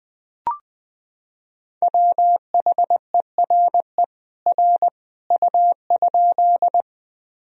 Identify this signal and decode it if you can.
Morse code, heard as a keyed tone, about 21 wpm, 700 Hz. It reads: WHERE R U?
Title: Sentences from Top 200 Words and Abbreviations 20wpm